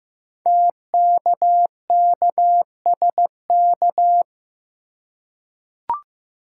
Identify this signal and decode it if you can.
Morse code, heard as a keyed tone, about 15 wpm, 700 Hz. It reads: TKKSK
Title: Code Groups - 5 Letters - 15wpm